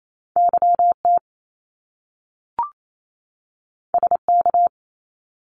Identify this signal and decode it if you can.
Morse code, heard as a keyed tone, about 28 wpm, 700 Hz. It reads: YT SK
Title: Canadian Provinces and Territories 28wpm